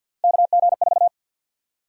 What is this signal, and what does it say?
Morse code, heard as a keyed tone, about 53 wpm, 700 Hz. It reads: KG4
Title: USA Call Sign Prefixes - 50wpm